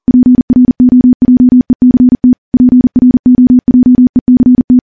Morse code, heard as a keyed tone, about 40 wpm, 250 Hz. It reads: PROJECT PROJECT
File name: morse / morse-training-40WPM